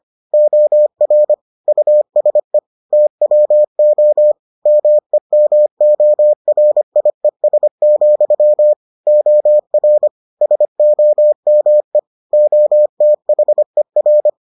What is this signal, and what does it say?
Morse code, heard as a keyed tone, about 25 wpm, 600 Hz. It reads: OR USE TWO MEMORIES, OR SOME OTHER